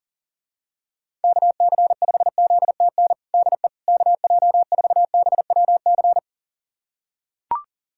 Morse code, heard as a keyed tone, about 40 wpm, 700 Hz. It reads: KC5ZTN DE KJ4BWC
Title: USA Callsign DE Callsign 40wpm